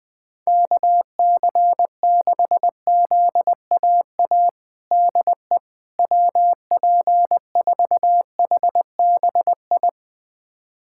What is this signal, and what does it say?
Morse code, heard as a keyed tone, about 21 wpm, 700 Hz. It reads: KC6ZAA DE WP4HBI
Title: USA Callsign DE Callsign - 6 Characters - 20wpm